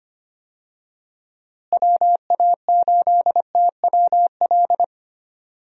Morse code, heard as a keyed tone, about 25 wpm, 700 Hz. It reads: WA8TWL
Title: USA Call Signs - 6 Characters - Repeated 3x - 25wpm